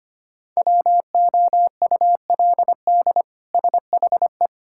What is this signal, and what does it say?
Morse code, heard as a keyed tone, about 25 wpm, 700 Hz. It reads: WOULD SHE
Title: Sentences from Top 1600 Words 25wpm